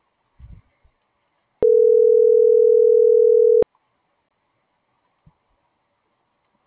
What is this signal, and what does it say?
Ringback tone (North America), 440 + 480 Hz.